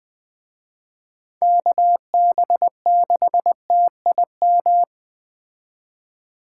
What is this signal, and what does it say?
Morse code, heard as a keyed tone, about 21 wpm, 700 Hz. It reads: KB6TIM
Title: Speed Racing - USA Call Signs 17wpm